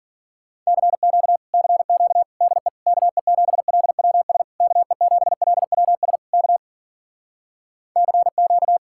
Morse code, heard as a keyed tone, about 52 wpm, 700 Hz. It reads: CQ CQ DE KE7LWS KE7LWS K CQ